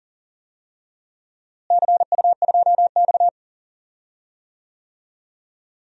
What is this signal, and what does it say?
Morse code, heard as a keyed tone, about 38 wpm, 700 Hz. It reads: CU2X
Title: DXCC-100 Call Signs - 4 Characters - Repeated 3x - 40wpm